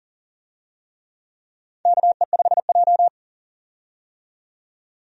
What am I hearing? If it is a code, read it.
Morse code, heard as a keyed tone, about 37 wpm, 700 Hz. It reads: KE5J
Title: USA Call Signs - 4 Characters - 40wpm